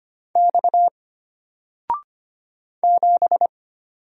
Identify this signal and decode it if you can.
Morse code, heard as a keyed tone, about 25 wpm, 700 Hz. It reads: X 7
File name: Single-Letter-Number-25wpm